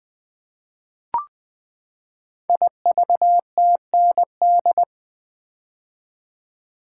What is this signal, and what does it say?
Morse code, heard as a keyed tone, about 20 wpm, 700 Hz. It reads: IVTND
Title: Code Groups - 5 Letters - 20wpm